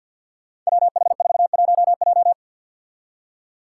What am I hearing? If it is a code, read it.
Morse code, heard as a keyed tone, about 51 wpm, 700 Hz. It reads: WH41J
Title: Five Challenging Letter-Number Combinations 50wpm